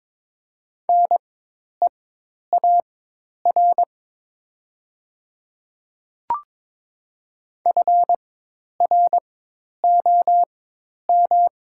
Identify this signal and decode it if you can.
Morse code, heard as a keyed tone, about 22 wpm, spaced as for 10 wpm, 700 Hz. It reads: NEAR FROM
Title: 4x Character Spacing - Top 500 Words - 22wpm